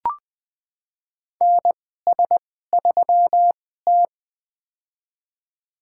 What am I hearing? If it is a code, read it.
Morse code, heard as a keyed tone, about 20 wpm, spaced as for 15 wpm, 700 Hz. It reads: NS3T
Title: USA Call Signs - 4 Characters - 2x Character Spacing - 20wpm